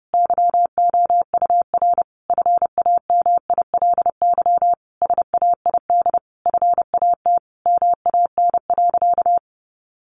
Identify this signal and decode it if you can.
Morse code, heard as a keyed tone, about 30 wpm, 700 Hz. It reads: YOUR FAMILY SAID FAT MAN.
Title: Sentences from Top 1100 Words 30wpm